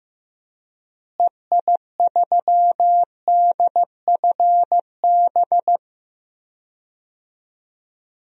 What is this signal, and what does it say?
Morse code, heard as a keyed tone, about 15 wpm, 700 Hz. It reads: EI3DFB
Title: DXCC-100 Call Signs - 6 Characters - Repeated 3x - 15wpm